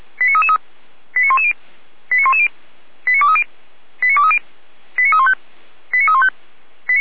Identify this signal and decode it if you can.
Five-tone selective call (ZVEI), heard as quick runs of five tones, tones of about 70 ms. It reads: 89333 89110 89110 89230 89230 89315 89315
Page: SDR/ZVEI